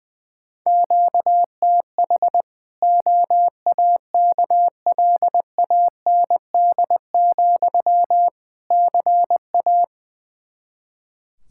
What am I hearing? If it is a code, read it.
Morse code, heard as a keyed tone, about 21 wpm, 700 Hz. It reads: QTH OAKLAND, CA
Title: K1USN SST QSO Practice 20wpm